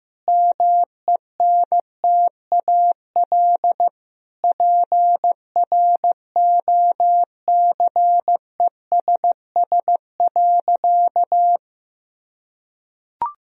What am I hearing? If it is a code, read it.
Morse code, heard as a keyed tone, about 15 wpm, 700 Hz. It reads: MENTAL PROCESS.
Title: Sentences from Top 1000 Words - Review Part 4 - 15wpm